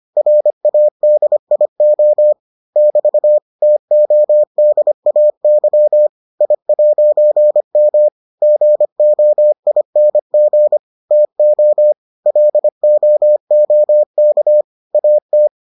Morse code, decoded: RADIO <BT> TODAY I'M GOING TO LOOK AT

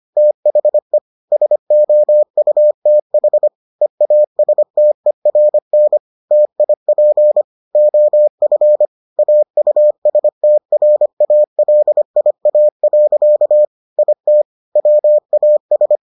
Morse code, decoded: THE SOUTH EASTERN TIP OF AUSTRALIA. IT WAS

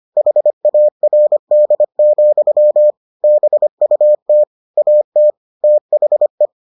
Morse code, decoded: HARD, BUT AT THE

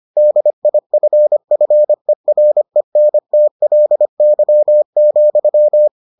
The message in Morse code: DIFFERENTLY,

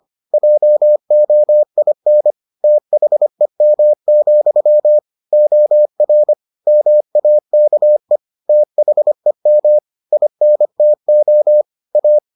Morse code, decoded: JOIN THEM, OR MAKE THEM INTO A

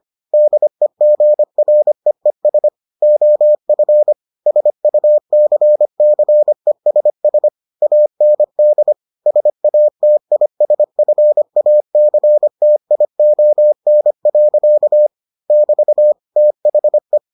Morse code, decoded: DEGREES OF SUCCESS AND SATISFACTION. <BT> THE

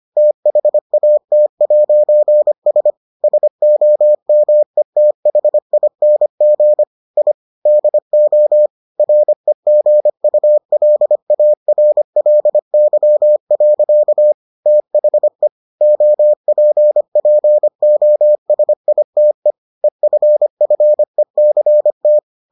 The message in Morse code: THAT'S SOMETHING I DO REGULARLY. THE OPPOSITE EFFECT

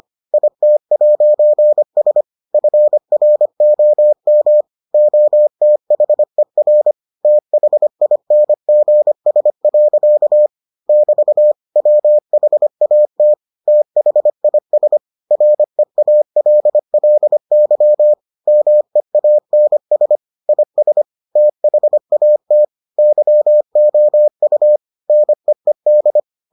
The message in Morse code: IT'S FROM OTHER THINGS. <BT> WHAT THIS REALLY MEANS IS THAT YOU NEED

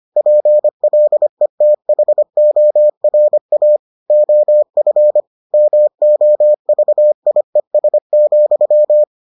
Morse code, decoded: PLETHORA OF MOVIES,

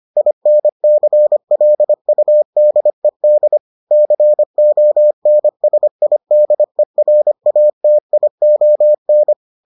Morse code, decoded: INCLUDED CONSIDERATION

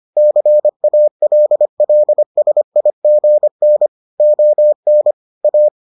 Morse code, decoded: CALLSIGN ON A

